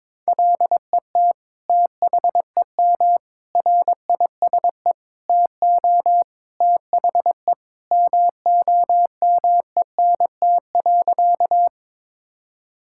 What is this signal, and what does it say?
Morse code, heard as a keyed tone, about 22 wpm, 700 Hz. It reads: LET THEM RISE TO THE MOMENT.
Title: Sentences from Top 1000 Words - Review Part 6 - 22wpm